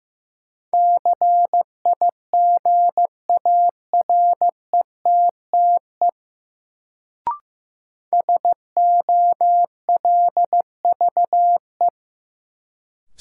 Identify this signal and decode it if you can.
Morse code, heard as a keyed tone, about 15 wpm, 700 Hz. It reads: CIGARETTE SOLVE
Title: Sentences from Top 1000 Words - New Word Drill - 15wpm